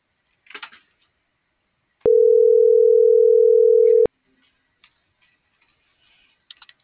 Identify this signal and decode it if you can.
Ringback tone (North America), 440 + 480 Hz.